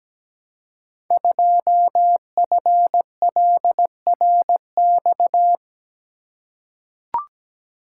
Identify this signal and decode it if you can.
Morse code, heard as a keyed tone, about 17 wpm, 700 Hz. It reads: 2FLRX
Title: Code Groups - 5 Letter:Numbers - 17wpm